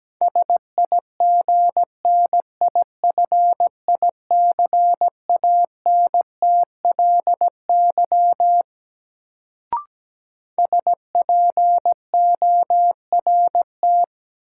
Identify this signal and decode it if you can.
Morse code, heard as a keyed tone, about 17 wpm, 700 Hz. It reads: SIGNIFICANTLY SPORT
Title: Sentences from Top 1500 Words - New Word Drill Alt - 17wpm